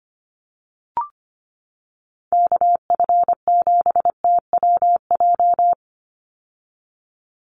Morse code, heard as a keyed tone, about 25 wpm, 700 Hz. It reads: KF7TWJ
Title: USA Call Signs 25wpm